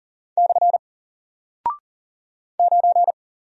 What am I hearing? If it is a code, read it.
Morse code, heard as a keyed tone, about 40 wpm, 700 Hz. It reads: / 9
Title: Single Character (Letter:Number:Punctuation:Prosign) Alt 40wpm